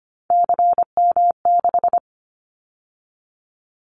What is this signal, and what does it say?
Morse code, heard as a keyed tone, about 25 wpm, 700 Hz. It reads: CM6
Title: Callsign Prefixes for Top 100 DXCC - 25wpm